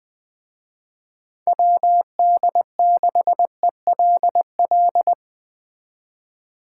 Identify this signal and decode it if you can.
Morse code, heard as a keyed tone, about 22 wpm, 700 Hz. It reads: WD6ELL
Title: Speed Racing - USA 2x3 Call Signs 20wpm